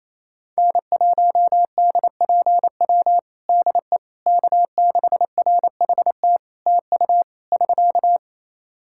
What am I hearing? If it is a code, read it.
Morse code, heard as a keyed tone, about 27 wpm, 700 Hz. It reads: N1DPW DE K6RHT TU <SK>
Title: Practice QSO - Exchange 4 - Ending - 28wpm